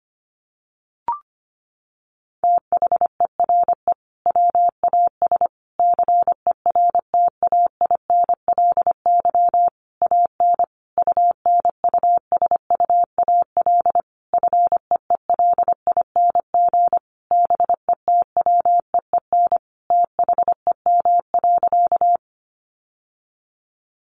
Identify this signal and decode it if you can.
Morse code, heard as a keyed tone, about 25 wpm, 700 Hz. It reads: THERE WAS CERTAINLY AN UNUSUAL FEELING BETWEEN THEM.
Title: Sentences from Top 1400 Words 25wpm